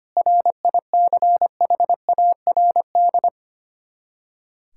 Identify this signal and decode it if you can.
Morse code, heard as a keyed tone, about 25 wpm, 700 Hz. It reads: RICHARD
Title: Practice QSO - Exchange 1 - Beginning - 25wpm